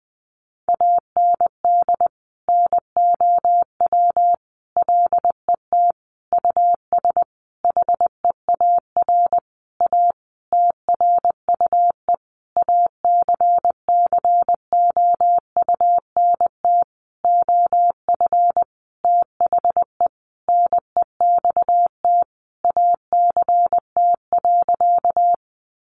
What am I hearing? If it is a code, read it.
Morse code, heard as a keyed tone, about 20 wpm, 700 Hz. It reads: AND NOW LET US HEAR A TRUE ACCOUNT OF THE NEXT ACT.
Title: Sentences from Top 900 Words 20wpm